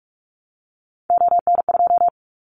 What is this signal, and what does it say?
Morse code, heard as a keyed tone, about 46 wpm, 700 Hz. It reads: ON2
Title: Callsign Prefixes for Top 100 DXCC - 45wpm